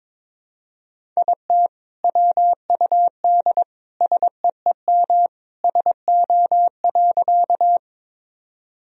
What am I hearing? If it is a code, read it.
Morse code, heard as a keyed tone, about 22 wpm, 700 Hz. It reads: IT WUD SEEM SO.
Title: Sentences from Top 300 Words and Abbreviations 22wpm